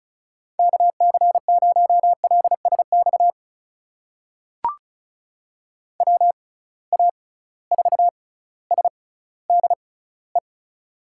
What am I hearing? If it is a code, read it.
Morse code, heard as a keyed tone, about 34 wpm, spaced as for 18 wpm, 700 Hz. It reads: KC0LSX WA4SDE
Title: USA Call Signs - 6 Characters - 6x Character Spacing - 35wpm